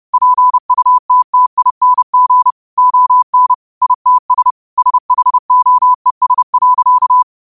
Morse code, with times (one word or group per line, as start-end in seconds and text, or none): 0.1-2.5 PUTTING
2.8-3.5 ON
3.8-4.5 ITS
4.8-7.2 SHOES.